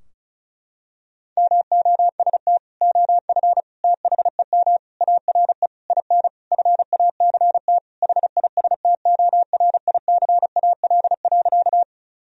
Morse code, heard as a keyed tone, about 36 wpm, 700 Hz. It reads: MOST OF THEM ARE IN FACT HISTORICAL.
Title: Sentences from Top 1000 Words - Review Part 4 - 35wpm